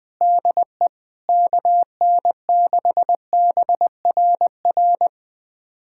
Morse code, heard as a keyed tone, about 21 wpm, 700 Hz. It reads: DE KN6BRR
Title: USA Callsign DE Callsign 20wpm